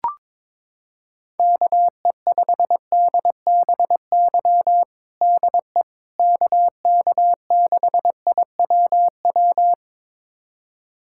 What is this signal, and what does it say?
Morse code, heard as a keyed tone, about 22 wpm, 700 Hz. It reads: KE5DBY DE KK6IWW
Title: USA Callsign DE Callsign - 6 Characters - 22wpm